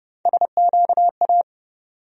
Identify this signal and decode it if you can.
Morse code, heard as a keyed tone, about 30 wpm, 700 Hz. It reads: SQA
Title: USA Call Sign Postfixes - 3 Letters 30wpm